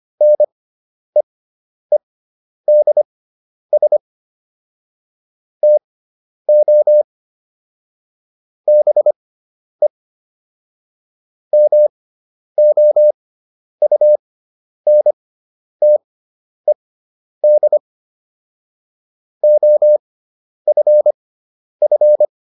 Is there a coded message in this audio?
Yes: Morse code from 0.2 to 22.3 s, heard as a keyed tone, about 25 wpm, spaced as for 10 wpm, 600 Hz. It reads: NEEDS TO BE MOUNTED OFF